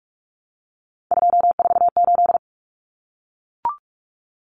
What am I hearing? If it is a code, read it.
Morse code, heard as a keyed tone, about 44 wpm, 700 Hz. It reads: 248
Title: Numbers - 3-Digit 45wpm